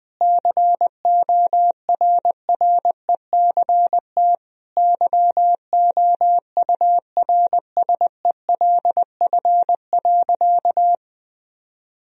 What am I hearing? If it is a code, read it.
Morse code, heard as a keyed tone, about 20 wpm, 700 Hz. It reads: CORRECT YOURSELF.
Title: Sentences from Top 1000 Words - Review Part 2 - 20wpm